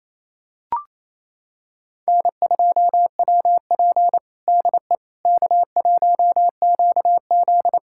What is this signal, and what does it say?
Morse code, heard as a keyed tone, about 27 wpm, 700 Hz. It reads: N2WP DE K1QZ
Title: USA Callsign DE Callsign - 4 Characters - 28wpm